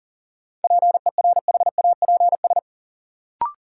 Morse code, heard as a keyed tone, about 37 wpm, 700 Hz. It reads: PERHAPS
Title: Sentences from Top 500 Words - New Word Drill - 40wpm